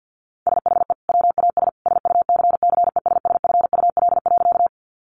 Morse code, heard as a keyed tone, about 50 wpm, 700 Hz. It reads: SHE WAS SUCCESSFUL.
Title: Sentences from Top 900 Words 50wpm